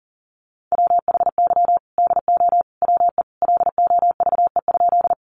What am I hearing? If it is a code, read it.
Morse code, heard as a keyed tone, about 40 wpm, 700 Hz. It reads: WHY DO WE LOVE?
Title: Sentences from Top 400 Words 40wpm